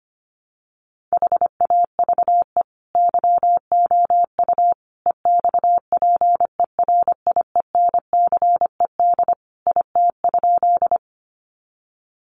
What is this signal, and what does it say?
Morse code, heard as a keyed tone, about 25 wpm, 700 Hz. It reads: HAVE YOU EXPERIENCED IT?
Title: Sentences from Top 1100 Words 25wpm